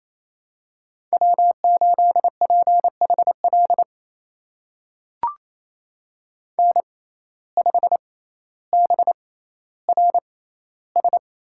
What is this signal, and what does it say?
Morse code, heard as a keyed tone, about 28 wpm, spaced as for 14 wpm, 700 Hz. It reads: W8PHL N5BRS
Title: USA Call Signs - 5 Characters - 6x Character Spacing - 28wpm